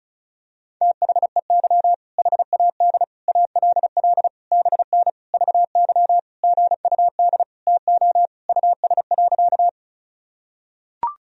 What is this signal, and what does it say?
Morse code, heard as a keyed tone, about 36 wpm, 700 Hz. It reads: THEY HAD ALL BN VY GUD TO US.
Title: Sentences from Top 200 Words and Abbreviations 35wpm